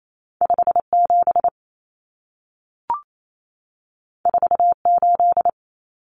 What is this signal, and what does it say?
Morse code, heard as a keyed tone, about 28 wpm, 700 Hz. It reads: 57 48
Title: Common Cut Numbers - 2-Digit 28wpm